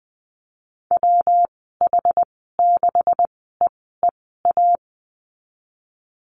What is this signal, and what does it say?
Morse code, heard as a keyed tone, about 20 wpm, spaced as for 15 wpm, 700 Hz. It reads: WH6EEA
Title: 147 Lesson 24g - Letter y - Call Signs 20wpm